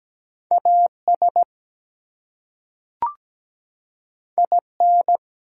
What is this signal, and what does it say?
Morse code, heard as a keyed tone, about 17 wpm, 700 Hz. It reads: AS IN
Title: Two Letter Words - 17wpm